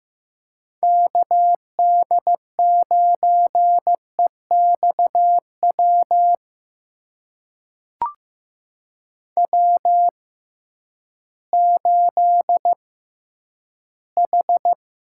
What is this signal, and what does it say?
Morse code, heard as a keyed tone, about 15 wpm, spaced as for 10 wpm, 700 Hz. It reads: KD9EXW W8H
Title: USA Call Signs - 6x Character Spacing - 15wpm